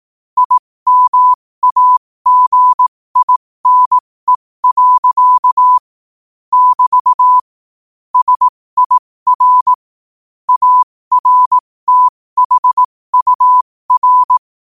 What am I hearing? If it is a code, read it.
Morse code, heard as a keyed tone, about 18 wpm, spaced as for 15 wpm, 1000 Hz. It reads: IMAGINE. <BT> SIR ARTHUR
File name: qotdfiles / 15wpm